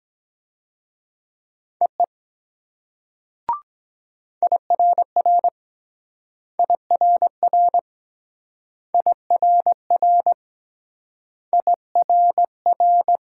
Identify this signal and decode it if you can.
Morse code, heard as a keyed tone, about 23 wpm, 700 Hz. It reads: EE IRR IRR IRR IRR